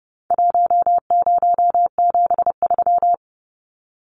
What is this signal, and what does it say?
Morse code, heard as a keyed tone, about 30 wpm, 700 Hz. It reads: 1073
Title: Numbers - 4-Digit 30wpm